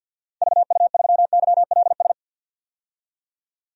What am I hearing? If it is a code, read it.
Morse code, heard as a keyed tone, about 52 wpm, 700 Hz. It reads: VU3YLS
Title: DXCC-100 Call Signs - 6 Characters - 50wpm